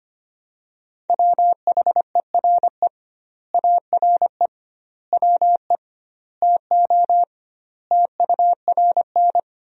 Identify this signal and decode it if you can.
Morse code, heard as a keyed tone, about 25 wpm, 700 Hz. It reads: WHERE ARE WE TO TURN
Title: 2x Word-Spacing - Sentences from Top 200 Words - 25wpm